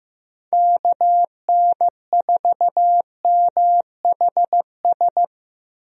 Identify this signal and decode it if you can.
Morse code, heard as a keyed tone, about 15 wpm, 700 Hz. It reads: KN4MHS
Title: USA Call Signs - 6 Characters - 15wpm